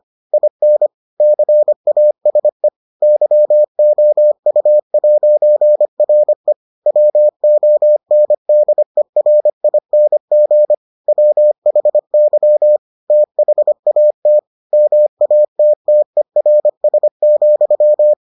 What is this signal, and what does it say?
Morse code, heard as a keyed tone, about 25 wpm, 600 Hz. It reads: IN CASE YOU'RE WONDERING WHY THAT MATTERS,